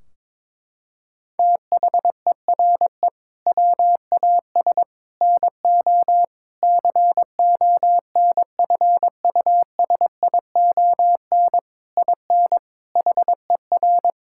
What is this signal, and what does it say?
Morse code, heard as a keyed tone, about 22 wpm, 700 Hz. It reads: THERE WAS NO CONFUSION IN HER
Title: Sentences from Top 1600 Words 22wpm